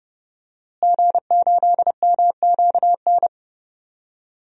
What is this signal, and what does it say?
Morse code, heard as a keyed tone, about 31 wpm, 700 Hz. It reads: G8MQN
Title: DXCC-100 Call Signs - 5 Characters - Repeated 3x - 30wpm